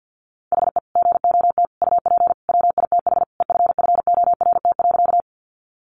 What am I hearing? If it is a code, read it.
Morse code, heard as a keyed tone, about 50 wpm, 700 Hz. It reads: HE GOT UP WITH EFFORT.
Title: Sentences from Top 1200 Words 50wpm